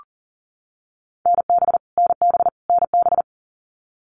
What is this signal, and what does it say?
Morse code, heard as a keyed tone, about 40 wpm, 700 Hz. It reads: NB NB NB
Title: Canadian Provinces and Territories - Repeated 3x - 40wpm